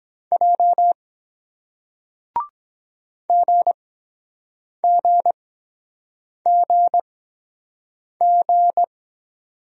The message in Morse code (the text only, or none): J GGGG